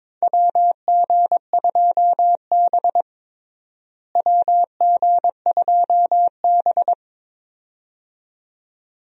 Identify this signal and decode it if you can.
Morse code, heard as a keyed tone, about 22 wpm, 700 Hz. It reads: WG2B WG2B
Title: USA Call Signs - 4 Characters - Repeated 2x - 22wpm